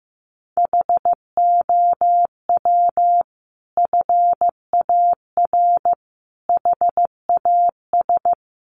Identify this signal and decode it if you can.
Morse code, heard as a keyed tone, about 15 wpm, 700 Hz. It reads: HOW FAR HAS
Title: Sentences from Top 1200 Words 15wpm